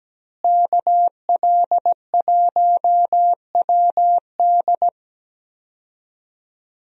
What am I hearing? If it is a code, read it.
Morse code, heard as a keyed tone, about 17 wpm, 700 Hz. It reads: KL1WD
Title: USA Call Signs - 5 Characters - Repeated 2x - 17wpm